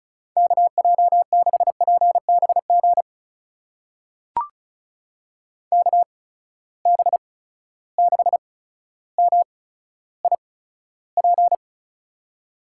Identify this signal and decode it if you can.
Morse code, heard as a keyed tone, about 36 wpm, spaced as for 15 wpm, 700 Hz. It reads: KJ6PBG KB6MIP
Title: USA Call Signs - 6 Characters - 8x Character Spacing - 35wpm